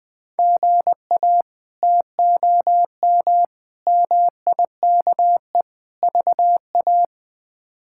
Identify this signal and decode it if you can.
Morse code, heard as a keyed tone, about 21 wpm, 700 Hz. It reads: GA TOM MIKE VA
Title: K1USN SST QSO Practice - 20wpm